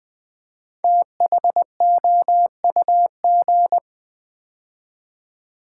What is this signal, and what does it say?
Morse code, heard as a keyed tone, about 20 wpm, 700 Hz. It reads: THOUG
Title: Build A Word - 5 Letters At A Time - 10-Letter Words - 20wpm